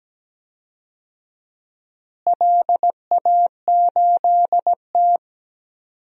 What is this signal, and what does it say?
Morse code, heard as a keyed tone, about 17 wpm, 700 Hz. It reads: LA8T
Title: DXCC-100 Call Signs - 4 Characters - Repeated 3x - 17wpm